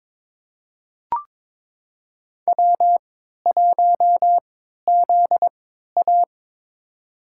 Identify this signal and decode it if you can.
Morse code, heard as a keyed tone, about 22 wpm, spaced as for 13 wpm, 700 Hz. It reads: W1ZA
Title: USA Call Signs - 3x Character Spacing - 22wpm